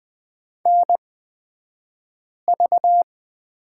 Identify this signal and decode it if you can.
Morse code, heard as a keyed tone, about 20 wpm, spaced as for 5 wpm, 700 Hz. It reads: NV
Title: 223 Lesson 36d - BT - Two Characters 20wpm